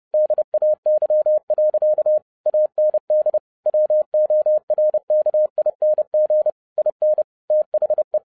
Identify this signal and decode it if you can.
Morse code, heard as a keyed tone, about 30 wpm, 610 Hz. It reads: DAY. AND WORKING IN THE
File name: 2011-07-mp3 / n60 BACK FROM A BIG ACCIDENT 30wpm